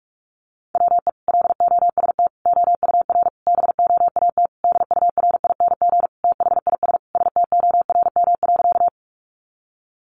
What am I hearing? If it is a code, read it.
Morse code, heard as a keyed tone, about 44 wpm, 700 Hz. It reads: WE LOST OUR BOAT DURING THIS STORM.
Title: Sentences from Top 1500 Words 45wpm